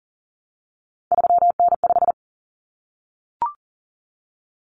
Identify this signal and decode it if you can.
Morse code, heard as a keyed tone, about 40 wpm, 700 Hz. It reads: 3N5
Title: Common Cut Numbers - 3-Digit 40wpm